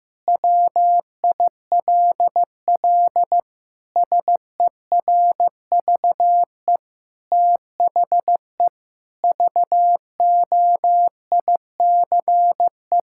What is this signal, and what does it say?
Morse code, heard as a keyed tone, about 15 wpm, 700 Hz. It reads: WILL SERVE THE VOICE
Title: Sentences from Top 1000 Words - Review Part 6 - 15wpm